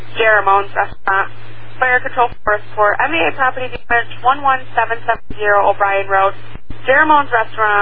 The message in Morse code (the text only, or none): none